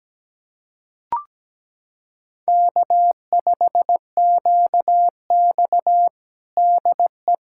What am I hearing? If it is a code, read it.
Morse code, heard as a keyed tone, about 17 wpm, 700 Hz. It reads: K5QX DE
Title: USA Callsign DE Callsign - 4 Characters - 17wpm